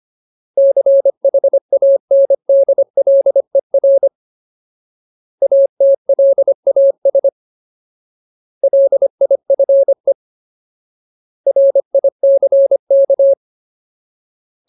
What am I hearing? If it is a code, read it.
Morse code, heard as a keyed tone, about 25 wpm, 550 Hz. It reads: CHANDLER ATLAS LIFE RICK
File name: Ultimate Word List / 25WPM Ultimate